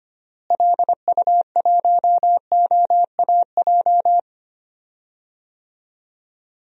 Morse code, heard as a keyed tone, about 25 wpm, 700 Hz. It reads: LU1OAJ
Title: DXCC-100 Call Signs - 6 Characters - Repeated 2x - 25wpm